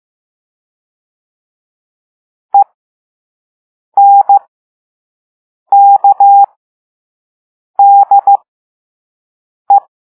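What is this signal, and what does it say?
Morse code, heard as a keyed tone, about 15 wpm, spaced as for 5 wpm, 800 Hz. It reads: ENKDE
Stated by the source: Morse code lesson 5 drill of random five-letter groups using all letters learned so far.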